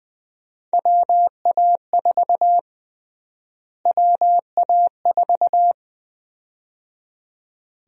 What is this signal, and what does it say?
Morse code, heard as a keyed tone, about 20 wpm, 700 Hz. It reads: WA4 WA4
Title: USA Call Sign Prefixes - Repeated 2x - 20wpm